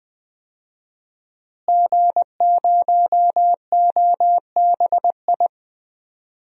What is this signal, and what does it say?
Morse code, heard as a keyed tone, about 20 wpm, 700 Hz. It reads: G0OBI
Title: DXCC-100 Call Signs - 5 Characters - Repeated 2x - 20wpm